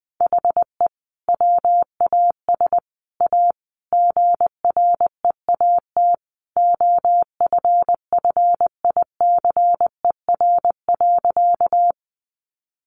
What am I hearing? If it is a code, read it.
Morse code, heard as a keyed tone, about 20 wpm, 700 Hz. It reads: HE WAS A GREAT OFFICER.
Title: Sentences from Top 1200 Words 20wpm